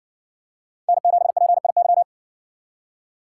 Morse code, heard as a keyed tone, about 60 wpm, 700 Hz. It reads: N7PIY